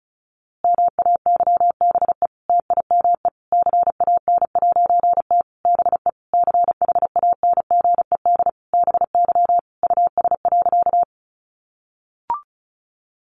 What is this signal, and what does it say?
Morse code, heard as a keyed tone, about 34 wpm, 700 Hz. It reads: MAYBE TIME CAN'T BE CHANGED BY US.
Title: A Sound Of Thunder 35wpm